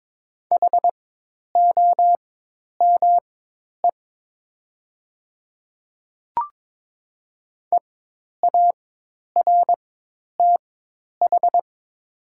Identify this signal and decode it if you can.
Morse code, heard as a keyed tone, about 22 wpm, spaced as for 10 wpm, 700 Hz. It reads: HOME EARTH
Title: 4x Character Spacing - Top 200 Words - 22wpm